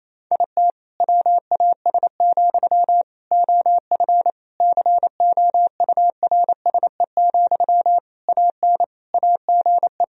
Morse code, heard as a keyed tone, about 28 wpm, 700 Hz. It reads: IT WAS, OF COURSE, AN AGE
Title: Sentences from Top 1000 Words - Review Part 3 - 28wpm